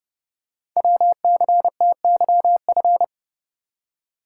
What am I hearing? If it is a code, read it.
Morse code, heard as a keyed tone, about 30 wpm, 700 Hz. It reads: WCTYF
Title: Code Groups - 5 Letter:Numbers - 30wpm